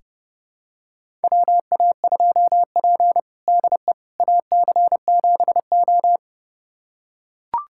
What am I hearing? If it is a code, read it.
Morse code, heard as a keyed tone, about 30 wpm, 700 Hz. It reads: WA2P DE AC7O
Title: USA Callsign DE Callsign - 4 Characters - 30wpm